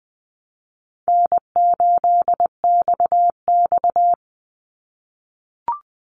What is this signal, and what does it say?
Morse code, heard as a keyed tone, about 20 wpm, 700 Hz. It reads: N8XX
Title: Easy Call Signs - 20wpm